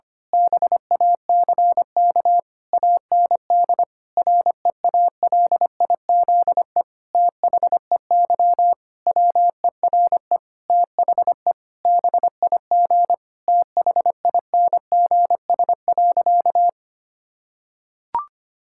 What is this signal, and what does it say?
Morse code, heard as a keyed tone, about 25 wpm, 700 Hz. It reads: BACK AND REALIZE THEY WERE THE BIG THINGS.
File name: practice / 25wpm / Difficult Sentences with Hints - 25wpm